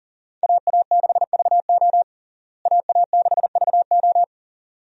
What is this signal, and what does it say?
Morse code, heard as a keyed tone, about 40 wpm, 700 Hz. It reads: AA6VO AA6VO
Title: USA Call Signs - 5 Characters - Repeated 2x - 40wpm